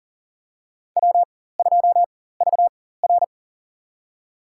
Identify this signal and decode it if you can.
Morse code, heard as a keyed tone, about 38 wpm, spaced as for 18 wpm, 700 Hz. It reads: W2VR
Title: USA Call Signs - 4 Characters - 4x Character Spacing - 40wpm